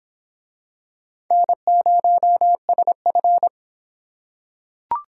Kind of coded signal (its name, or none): Morse code